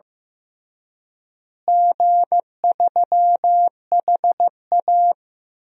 Morse code, heard as a keyed tone, about 15 wpm, 700 Hz. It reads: G3HA
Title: DXCC-100 Call Signs - 4 Characters - Repeated 3x - 15wpm